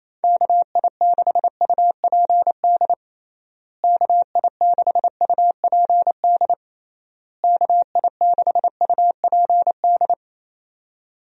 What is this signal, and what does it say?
Morse code, heard as a keyed tone, about 28 wpm, 700 Hz. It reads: KI6UPD KI6UPD KI6UPD
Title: USA Call Signs - 6 Characters - Repeated 3x - 28wpm